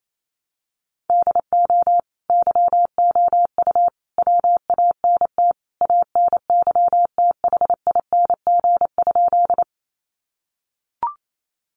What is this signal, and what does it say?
Morse code, heard as a keyed tone, about 28 wpm, 700 Hz. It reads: DO YOU WANT ANYTHING?
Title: Sentences from Top 1000 Words 28wpm